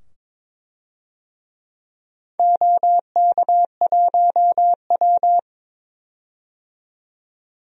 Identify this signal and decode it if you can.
Morse code, heard as a keyed tone, about 23 wpm, 700 Hz. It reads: OK1W
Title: DXCC-100 Call Signs - 4 Characters - 22wpm